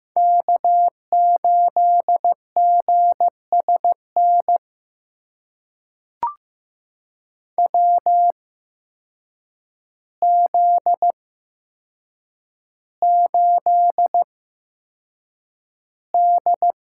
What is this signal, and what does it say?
Morse code, heard as a keyed tone, about 15 wpm, spaced as for 7 wpm, 700 Hz. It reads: K8GSN WZ8D
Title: USA Call Signs - 8x Character Spacing - 15wpm